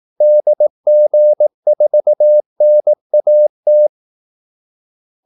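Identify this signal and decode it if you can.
Morse code, heard as a keyed tone, about 18 wpm, 600 Hz. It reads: DG4NAT
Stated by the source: Morse call-sign practice at 18 wpm.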